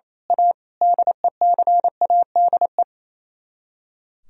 Morse code, heard as a keyed tone, about 26 wpm, 700 Hz. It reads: A DECADE
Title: Speed-Racing - Sets of 5 Words 28wpm